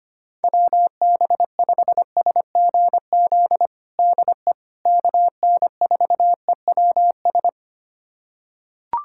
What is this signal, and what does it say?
Morse code, heard as a keyed tone, about 25 wpm, 700 Hz. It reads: WB5SGZ DE KN4EWS
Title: USA Callsign DE Callsign 25wpm